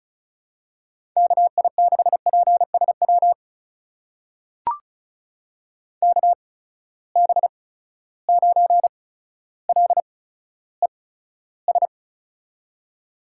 Morse code, heard as a keyed tone, about 35 wpm, spaced as for 15 wpm, 700 Hz. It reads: KI6PSW KB9LES
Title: USA Call Signs - 6 Characters - 8x Character Spacing - 35wpm